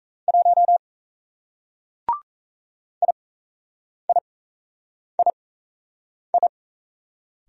Morse code, heard as a keyed tone, about 38 wpm, spaced as for 8 wpm, 700 Hz. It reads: 1 IIII